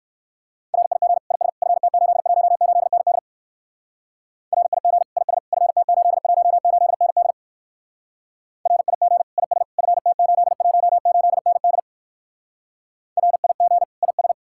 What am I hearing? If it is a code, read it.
Morse code, heard as a keyed tone, about 62 wpm, 700 Hz. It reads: RIG IS FT818ND RIG IS FT818ND RIG IS FT818ND RIG IS